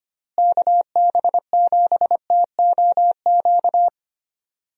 Morse code, heard as a keyed tone, about 25 wpm, 700 Hz. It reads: KB7TOQ
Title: USA Call Signs - 6 Characters - Repeated 2x - 25wpm